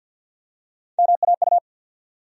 Morse code, heard as a keyed tone, about 48 wpm, 700 Hz. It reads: MAU
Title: USA Call Sign Postfixes - 3 Letters 50wpm